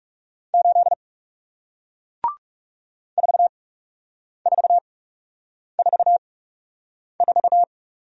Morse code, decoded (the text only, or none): 8 4444